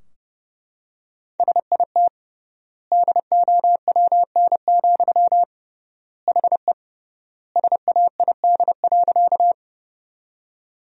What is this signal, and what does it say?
Morse code, heard as a keyed tone, about 30 wpm, 700 Hz. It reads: SIT DOWN, HE SAID.
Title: 3x Word-Spacing - Sentences from Top 500 Words - 30wpm